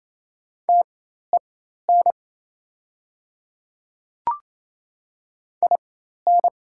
Morse code, heard as a keyed tone, about 27 wpm, spaced as for 13 wpm, 700 Hz. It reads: TEN IN
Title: 4x Character Spacing - Top 500 Words - 28wpm